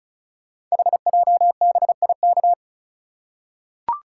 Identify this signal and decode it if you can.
Morse code, heard as a keyed tone, about 36 wpm, 690 Hz. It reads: HJBIK
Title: Code Groups - 5 Letters - 35wpm